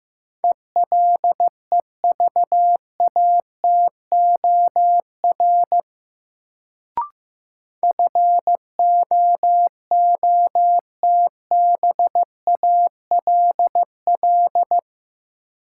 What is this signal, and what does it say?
Morse code, heard as a keyed tone, about 15 wpm, 700 Hz. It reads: ELEVATOR FOOTBALL
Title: Sentences from Top 1600 Words - New Word Drill - 15wpm